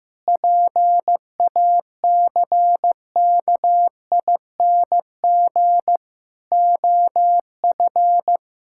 Morse code, heard as a keyed tone, about 16 wpm, 700 Hz. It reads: PACKING OF
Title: Pangram Challenge v2 - 15wpm